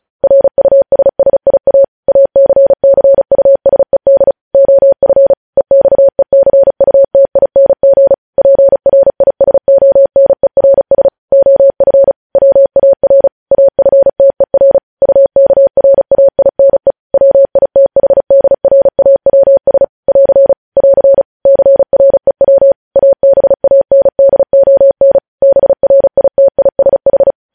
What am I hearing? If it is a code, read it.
Morse code, heard as a keyed tone, about 36 wpm, 550 Hz. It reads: RUSSIA ACCUSED OF EXECUTING PRISONERS OF WAR AFTER UKRAINE WITHDRAWS <AR> <AR> CREW ABANDON BRITISH